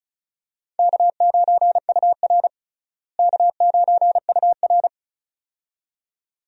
Morse code, heard as a keyed tone, about 36 wpm, 700 Hz. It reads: K9UR K9UR
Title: USA Call Signs - 4 Characters - Repeated 2x - 35wpm